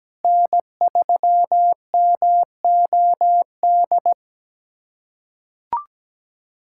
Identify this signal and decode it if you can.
Morse code, heard as a keyed tone, about 17 wpm, 700 Hz. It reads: N3MOD
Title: USA Callsign DE Callsign - 5 Characters - 17wpm